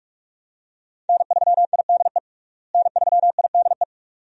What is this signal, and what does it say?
Morse code, heard as a keyed tone, about 46 wpm, 700 Hz. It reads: N3IDE N3IDE
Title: USA Call Signs - 5 Characters - Repeated 2x - 45wpm